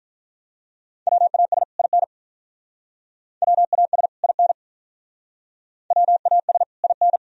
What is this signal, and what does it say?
Morse code, heard as a keyed tone, about 42 wpm, 700 Hz. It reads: WAS IN WAS IN WAS IN
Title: Speed-Racing - Sets of 2 Words 35wpm